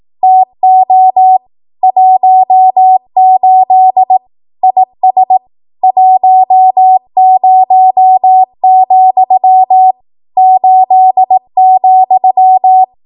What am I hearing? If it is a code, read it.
Morse code, heard as a keyed tone, about 18 wpm, 750 Hz. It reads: TO 18 IS 10, 8,
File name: Morse / 18WPM